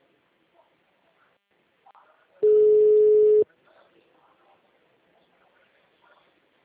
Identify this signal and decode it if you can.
Ringback tone (Europe, 425 Hz), with speech.